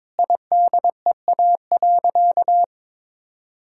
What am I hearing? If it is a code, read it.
Morse code, heard as a keyed tone, about 22 wpm, 700 Hz. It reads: IDEA.
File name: practice / 22wpm / Sentences with 2-Word Phrases - 22wpm